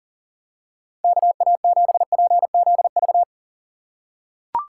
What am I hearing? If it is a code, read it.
Morse code, heard as a keyed tone, about 37 wpm, 700 Hz. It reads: KA7PZV
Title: USA Call Signs - 6 Characters - 6x Character Spacing - 40wpm